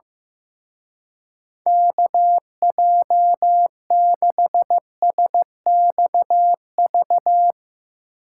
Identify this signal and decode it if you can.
Morse code, heard as a keyed tone, about 16 wpm, 700 Hz. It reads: KJ6SXV
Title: Speed Racing - USA Call Signs 15wpm